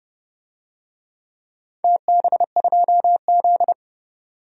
Morse code, heard as a keyed tone, about 30 wpm, 700 Hz. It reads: TB2Z
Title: DXCC-100 Call Signs - 4 Characters - 30wpm